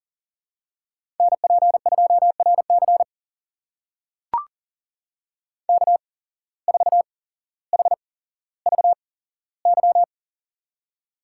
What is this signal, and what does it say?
Morse code, heard as a keyed tone, about 40 wpm, spaced as for 17 wpm, 700 Hz. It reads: NP2RC K4HVY
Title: USA Call Signs - 5 Characters - 8x Character Spacing - 40wpm